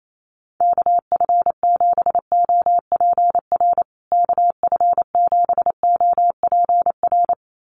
Morse code, heard as a keyed tone, about 28 wpm, 700 Hz. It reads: KF7OPR KF7OPR
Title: USA Call Signs Repeated 2x 28wpm